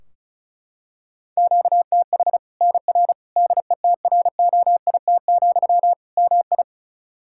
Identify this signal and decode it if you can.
Morse code, heard as a keyed tone, about 34 wpm, 700 Hz. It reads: QTH NR DETROIT, MI
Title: K1USN SST QSO Practice 35wpm